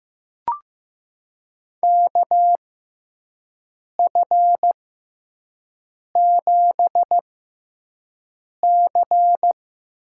Morse code, heard as a keyed tone, about 15 wpm, spaced as for 5 wpm, 700 Hz. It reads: KF7C
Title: USA Call Signs - 6x Character Spacing - 15wpm